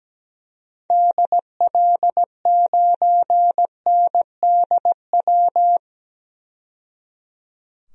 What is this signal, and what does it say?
Morse code, heard as a keyed tone, about 17 wpm, 700 Hz. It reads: DL9NDW
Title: DXCC-100 Call Signs - 6 Characters - 17wpm